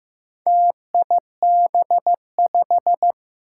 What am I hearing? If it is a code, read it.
Morse code, heard as a keyed tone, about 15 wpm, 700 Hz. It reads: TIB5
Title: Four Challenging Letter-Number Combinations 15wpm